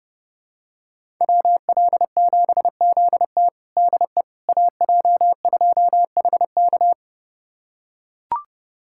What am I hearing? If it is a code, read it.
Morse code, heard as a keyed tone, about 32 wpm, 700 Hz. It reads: WL7ZT DE AJ2HK
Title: USA Callsign DE Callsign - 5 Characters - 30wpm